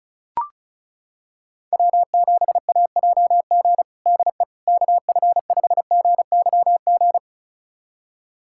Morse code, heard as a keyed tone, about 36 wpm, 700 Hz. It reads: W7AJG DE KF5GYG
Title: USA Callsign DE Callsign 35wpm